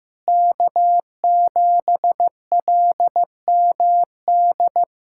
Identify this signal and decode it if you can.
Morse code, heard as a keyed tone, about 15 wpm, 700 Hz. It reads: K7LMD
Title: USA Callsign DE Callsign - 5 Characters - 15wpm